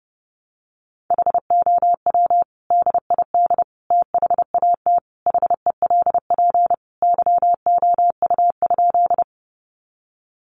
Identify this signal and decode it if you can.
Morse code, heard as a keyed tone, about 30 wpm, 700 Hz. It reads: HOW DID THAT HELP YOU?
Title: Sentences from Top 200 Words 30wpm